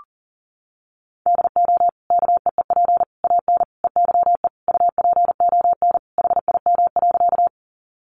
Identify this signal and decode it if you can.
Morse code, heard as a keyed tone, about 40 wpm, 700 Hz. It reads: DO KEEP AN EYE UPON HIM.
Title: Sentences from Top 1100 Words 40wpm